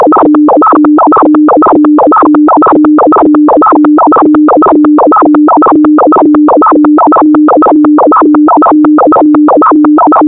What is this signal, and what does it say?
Two-tone teleprinter signal, about 400 Hz shift.
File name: thet / radia297 / radia s24 n297 radiohelsinki-raggam--dataradio-pt4